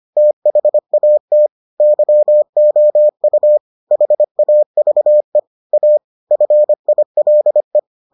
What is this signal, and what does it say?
Morse code, heard as a keyed tone, about 25 wpm, 600 Hz. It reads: THAT YOU HAVE A FILE